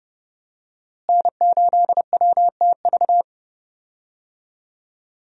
Morse code, heard as a keyed tone, about 30 wpm, 700 Hz. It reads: N8WTV
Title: USA Call Signs - 5 Characters - Repeated 2x - 30wpm